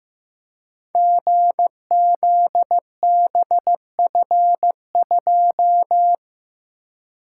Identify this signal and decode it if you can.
Morse code, heard as a keyed tone, about 15 wpm, 700 Hz. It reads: GZBF2
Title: Code Groups - 5 Letter:Numbers - 15wpm